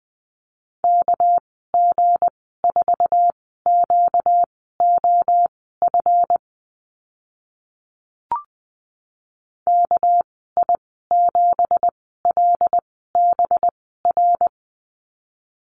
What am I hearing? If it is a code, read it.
Morse code, heard as a keyed tone, about 20 wpm, spaced as for 14 wpm, 700 Hz. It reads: KG4QOF KI7LBR
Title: 238 Lesson 38g - Letter q - Call Signs 20wpm